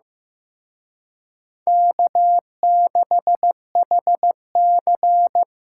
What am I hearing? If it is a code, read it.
Morse code, heard as a keyed tone, about 17 wpm, 700 Hz. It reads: K6HC
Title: Speed Racing - USA 1x2 Call Signs 15wpm